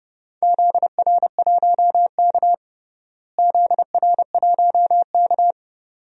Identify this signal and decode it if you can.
Morse code, heard as a keyed tone, about 32 wpm, 700 Hz. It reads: ZR1K ZR1K
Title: DXCC-100 Call Signs - 4 Characters - Repeated 2x - 30wpm